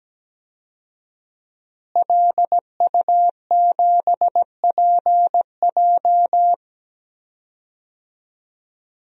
Morse code, heard as a keyed tone, about 17 wpm, 700 Hz. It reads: LU7PJ
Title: DXCC-100 Call Signs - 5 Characters - Repeated 3x - 17wpm